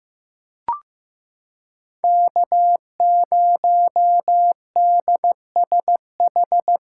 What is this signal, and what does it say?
Morse code, heard as a keyed tone, about 15 wpm, 700 Hz. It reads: K0DSH
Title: USA Callsign DE Callsign - 5 Characters - 15wpm